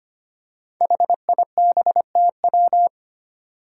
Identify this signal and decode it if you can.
Morse code, heard as a keyed tone, about 25 wpm, 700 Hz. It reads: HIBTW
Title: Five Challenging Letter-Number Combinations 25wpm